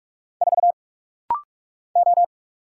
Morse code, heard as a keyed tone, about 46 wpm, 700 Hz. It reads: 4 O
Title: Single Letter-Number Alt - Rapid-Fire 45wpm